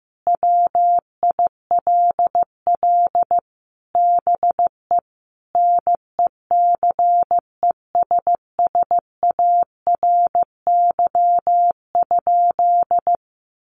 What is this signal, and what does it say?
Morse code, heard as a keyed tone, about 15 wpm, 700 Hz. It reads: WILL BE NECESSARY?
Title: Sentences from Top 800 Words 15wpm